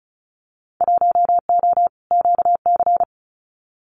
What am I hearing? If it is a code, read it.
Morse code, heard as a keyed tone, about 34 wpm, 700 Hz. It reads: 1O QC
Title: Winter Field Day Exchanges 35wpm